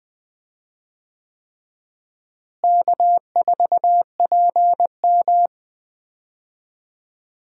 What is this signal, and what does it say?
Morse code, heard as a keyed tone, about 20 wpm, 700 Hz. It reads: K4PM
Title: USA Call Signs - 4 Characters - Repeated 3x - 20wpm